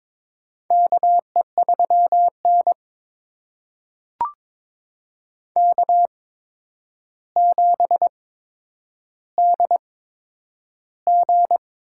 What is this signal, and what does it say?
Morse code, heard as a keyed tone, about 22 wpm, spaced as for 9 wpm, 700 Hz. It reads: KE3N K7DG
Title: USA Call Signs - 4 Characters - 8x Character Spacing - 22wpm